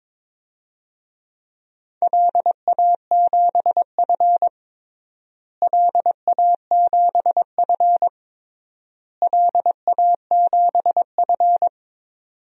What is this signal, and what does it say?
Morse code, heard as a keyed tone, about 22 wpm, 700 Hz. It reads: LA7F LA7F LA7F
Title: DXCC-100 Call Signs - 4 Characters - Repeated 3x - 22wpm